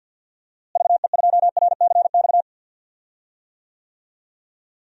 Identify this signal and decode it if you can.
Morse code, heard as a keyed tone, about 51 wpm, 700 Hz. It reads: VE2RCX
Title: DXCC-100 Call Signs - 6 Characters - Repeated 3x - 50wpm